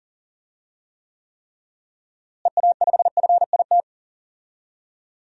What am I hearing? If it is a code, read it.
Morse code, heard as a keyed tone, about 40 wpm, 700 Hz. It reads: EA5FIT
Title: DXCC-100 Call Signs - 6 Characters - Repeated 3x - 40wpm